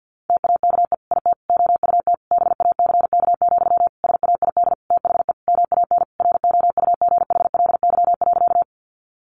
Morse code, heard as a keyed tone, about 50 wpm, 700 Hz. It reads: TAKE IT OUT BACK, SAID THE MAN ROUGHLY.
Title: Sentences from Top 1500 Words 50wpm